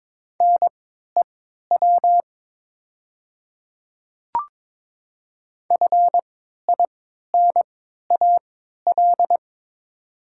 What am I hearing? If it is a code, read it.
Morse code, heard as a keyed tone, about 22 wpm, spaced as for 12 wpm, 700 Hz. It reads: NEW FINAL
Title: 3x Character Spacing - Top 500 Words - 22wpm